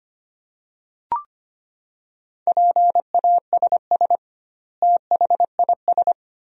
Morse code, heard as a keyed tone, about 25 wpm, 700 Hz. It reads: PASS THIS
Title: 2x Word-Spacing - Sentences from Top 400 Words - 25wpm